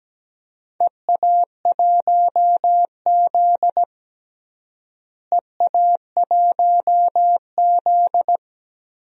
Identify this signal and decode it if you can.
Morse code, heard as a keyed tone, about 17 wpm, 700 Hz. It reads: EA1Z EA1Z
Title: DXCC-100 Call Signs - 4 Characters - Repeated 3x - 17wpm